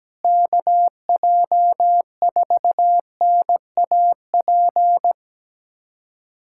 Morse code, heard as a keyed tone, about 17 wpm, 700 Hz. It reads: KJ4NAP
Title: USA Callsign DE Callsign - 6 Characters - 17wpm